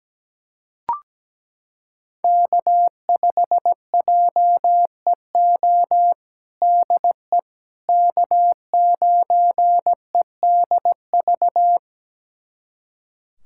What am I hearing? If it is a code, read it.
Morse code, heard as a keyed tone, about 17 wpm, 700 Hz. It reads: K5JEO DE K9EDV
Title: USA Callsign DE Callsign - 5 Characters - 17wpm